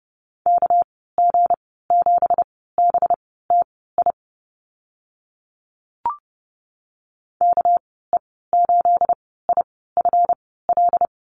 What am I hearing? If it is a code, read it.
Morse code, heard as a keyed tone, about 30 wpm, spaced as for 17 wpm, 700 Hz. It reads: KG7BTI KE8IFL
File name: Lesson 37g - Letter x - Call Signs 30wpm